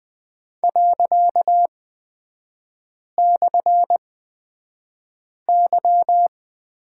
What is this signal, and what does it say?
Morse code, heard as a keyed tone, about 20 wpm, spaced as for 5 wpm, 700 Hz. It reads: ./Y
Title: 255 Lesson 41e - Period - Three Characters 20wpm